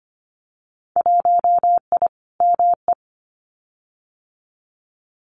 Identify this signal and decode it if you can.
Morse code, heard as a keyed tone, about 25 wpm, 700 Hz. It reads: 1I ME
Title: Winter Field Day Exchanges Encore 25wpm